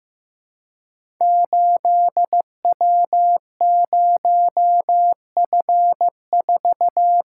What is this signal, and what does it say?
Morse code, heard as a keyed tone, about 15 wpm, 700 Hz. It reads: 8W0F4
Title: Code Groups - 5 Letter:Numbers - 15wpm